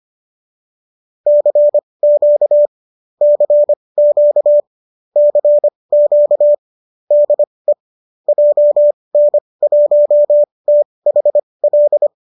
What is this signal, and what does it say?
Morse code, heard as a keyed tone, about 25 wpm, spaced as for 20 wpm, 590 Hz. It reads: CQ CQ CQ DE JN1THL
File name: qso205 20